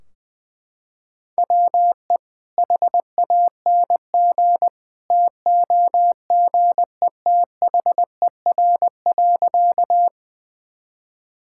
Morse code, heard as a keyed tone, about 21 wpm, 700 Hz. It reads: WE HANG TOGETHER.
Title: Sentences from Top 1600 Words 20wpm